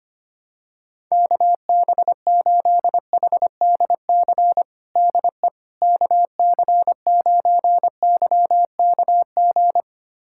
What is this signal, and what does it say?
Morse code, heard as a keyed tone, about 25 wpm, 700 Hz. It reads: KB8HDC DE KC9YKG
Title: USA Callsign DE Callsign - 6 Characters - 25wpm